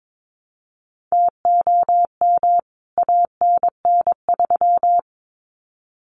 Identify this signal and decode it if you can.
Morse code, heard as a keyed tone, about 22 wpm, 700 Hz. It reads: TOM ANN3
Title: CWT Exchange 22wpm